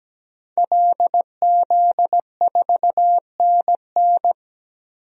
Morse code, heard as a keyed tone, about 17 wpm, 700 Hz. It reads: LZ4NN
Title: DXCC-100 Call Signs - 5 Characters - 17wpm